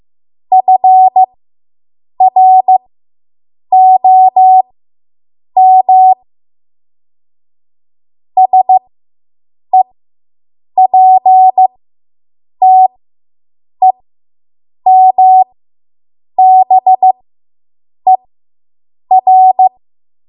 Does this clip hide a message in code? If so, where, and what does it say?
Yes: Morse code from 0.5 to 19.7 s, heard as a keyed tone, about 15 wpm, spaced as for 7 wpm, 750 Hz. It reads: FROM SEPTEMBER